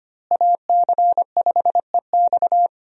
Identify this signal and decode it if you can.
Morse code, heard as a keyed tone, about 25 wpm, 700 Hz. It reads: AC5EX
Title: USA Call Signs - 5 Characters - Repeated 2x - 25wpm